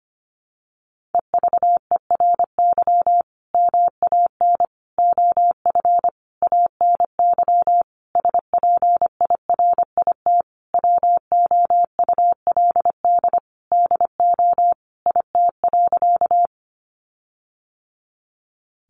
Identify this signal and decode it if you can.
Morse code, heard as a keyed tone, about 25 wpm, 700 Hz. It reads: EVERY MAN OF ANY SPIRIT WOULD DO IT.
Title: Sentences from Top 1400 Words 25wpm